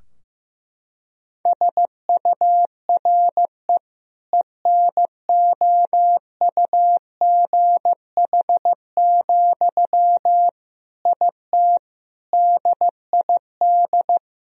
Morse code, heard as a keyed tone, about 15 wpm, 700 Hz. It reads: SURE ENOUGH, IT DID
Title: Sentences from Top 1000 Words - Review Part 2 - 15wpm